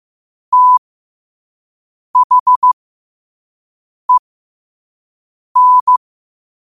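Morse code, heard as a keyed tone, about 15 wpm, spaced as for 5 wpm, 1000 Hz. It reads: THEN
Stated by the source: 5 wpm Morse.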